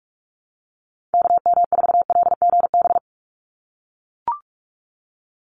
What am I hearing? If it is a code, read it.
Morse code, heard as a keyed tone, about 44 wpm, 700 Hz. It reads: KM4LGB
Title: USA Call Signs 45wpm